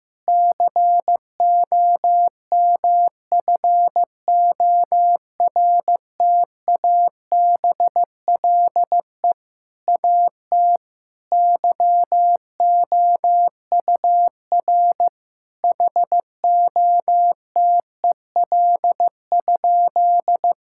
Morse code, heard as a keyed tone, about 15 wpm, 700 Hz. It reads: COMFORTABLE AT YOUR HOTEL?
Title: Sentences from Top 1600 Words 15wpm